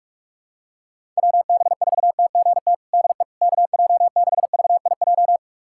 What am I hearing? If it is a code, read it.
Morse code, heard as a keyed tone, about 42 wpm, 700 Hz. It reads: WB4TGT DE KJ6VIJ
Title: USA Callsign DE Callsign - 6 Characters - 45wpm